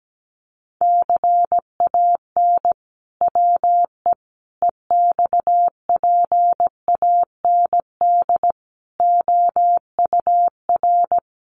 Morse code, heard as a keyed tone, about 17 wpm, 700 Hz. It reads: CAN WE EXPAND OUR
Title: Sentences from Top 1400 Words 17wpm